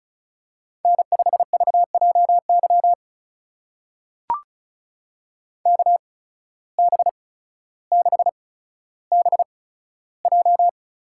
Morse code, heard as a keyed tone, about 34 wpm, spaced as for 15 wpm, 700 Hz. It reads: N5VJY KB6BJ
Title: USA Call Signs - 5 Characters - 8x Character Spacing - 35wpm